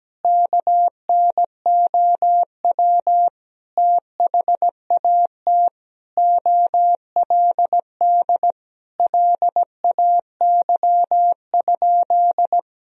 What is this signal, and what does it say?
Morse code, heard as a keyed tone, about 17 wpm, 700 Hz. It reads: KNOW THAT OLD LAY?
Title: Sentences from Top 400 Words and Abbreviations 17wpm